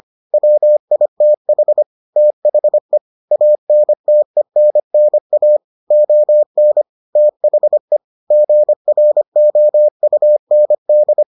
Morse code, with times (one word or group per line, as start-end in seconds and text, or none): 0.3-1.8 WITH
2.2-3.0 THE
3.3-5.6 ANTENNA
5.9-6.8 ON
7.2-8.0 THE
8.3-11.2 GROUND